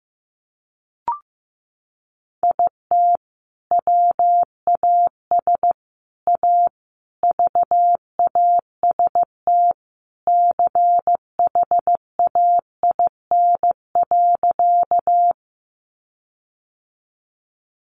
Morse code, heard as a keyed tone, about 15 wpm, 700 Hz. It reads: IT WAS A VAST CHAIN.
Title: Sentences from Top 1500 Words 15wpm